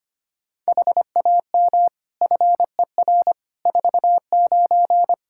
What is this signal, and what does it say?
Morse code, heard as a keyed tone, about 25 wpm, 700 Hz. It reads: HAM FER 49
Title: K1USN SST QSO Practice 25wpm